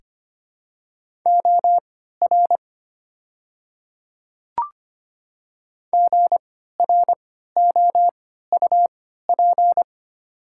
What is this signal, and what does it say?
Morse code, heard as a keyed tone, about 25 wpm, spaced as for 14 wpm, 690 Hz. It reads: OR GROUP
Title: 3x Character Spacing - Top 500 Words - 25wpm